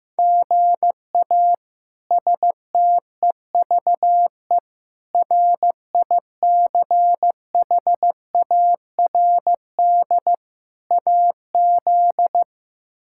Morse code, heard as a keyed tone, about 16 wpm, 700 Hz. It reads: GA STEVE RICHARD AZ
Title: K1USN SST QSO Practice - 15wpm